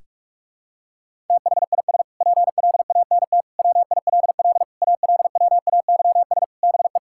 Morse code, heard as a keyed tone, about 46 wpm, 700 Hz. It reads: THIS PLANT WILL ALWAYS BE
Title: Sentences from Top 1000 Words - Review Part 6 - 45wpm